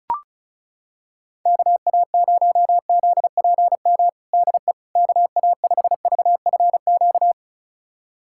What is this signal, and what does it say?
Morse code, heard as a keyed tone, about 36 wpm, 700 Hz. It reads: KA0ZPM DE KA5VFQ
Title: USA Callsign DE Callsign - 6 Characters - 35wpm